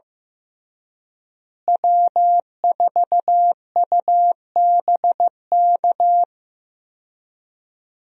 Morse code, heard as a keyed tone, about 15 wpm, 700 Hz. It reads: W4UBK
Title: USA Call Signs - 5 Characters - Repeated 3x - 15wpm